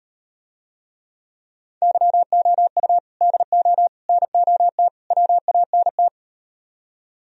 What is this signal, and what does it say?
Morse code, heard as a keyed tone, about 39 wpm, 700 Hz. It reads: YOU DO NOT WANT